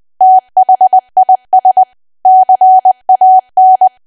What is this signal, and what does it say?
Morse code, heard as a keyed tone, about 20 wpm, 750 Hz. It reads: THIS CAN
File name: cwa / W1AW-Level3 / 20WPM